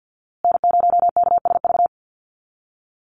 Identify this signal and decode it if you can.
Morse code, heard as a keyed tone, about 50 wpm, 700 Hz. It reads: N0KSV
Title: USA Call Signs Repeated 2x 50wpm